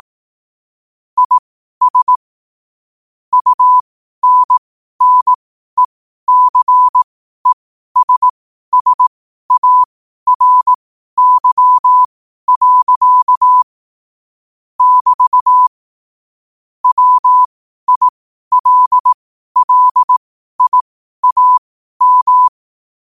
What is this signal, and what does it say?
Morse code, heard as a keyed tone, about 18 wpm, spaced as for 12 wpm, 1000 Hz. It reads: IS UNNECESSARY. <BT> WILLIAM
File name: qotdfiles / 12wpm